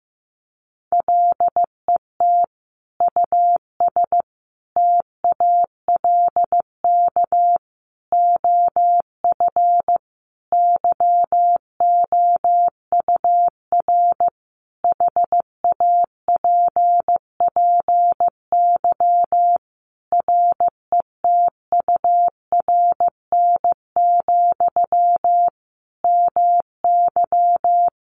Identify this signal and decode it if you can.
Morse code, heard as a keyed tone, about 15 wpm, 700 Hz. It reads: LET US TALK OF YOUR HAPPY RETURN, MY
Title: Sentences from Top 1000 Words 15wpm